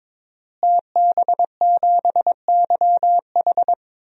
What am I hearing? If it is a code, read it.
Morse code, heard as a keyed tone, about 22 wpm, 700 Hz. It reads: TB7YH
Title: DXCC-100 Call Signs - 5 Characters - Repeated 3x - 22wpm